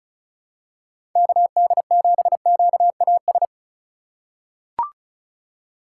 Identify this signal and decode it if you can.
Morse code, heard as a keyed tone, about 36 wpm, 700 Hz. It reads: KD7QAS
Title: USA Call Signs - 6 Characters - 4x Character Spacing - 35wpm